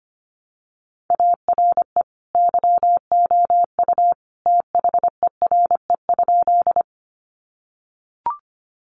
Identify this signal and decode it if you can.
Morse code, heard as a keyed tone, about 25 wpm, 700 Hz. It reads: ARE YOU THERE?
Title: Sentences from Top 100 Words 25wpm